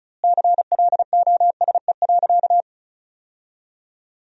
Morse code, heard as a keyed tone, about 36 wpm, 700 Hz. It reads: CLOSE.
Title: 2x Word-Spacing - Sentences from Top 300 Words - 35wpm